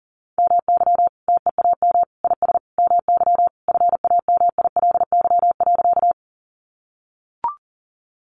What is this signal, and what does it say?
Morse code, heard as a keyed tone, about 40 wpm, 700 Hz. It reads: MY TEAM IS MY FAMILY.
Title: Sentences from Top 1500 Words 40wpm